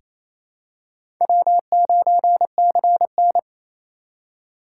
Morse code, heard as a keyed tone, about 28 wpm, 700 Hz. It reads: W9CN
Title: USA Call Signs - 4 Characters - 2x Character Spacing - 28wpm